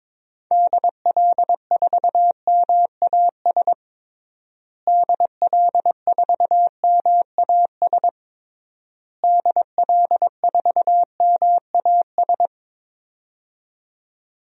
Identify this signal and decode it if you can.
Morse code, heard as a keyed tone, about 22 wpm, 700 Hz. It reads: DL4MAS DL4MAS DL4MAS
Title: DXCC-100 Call Signs - 6 Characters - Repeated 3x - 22wpm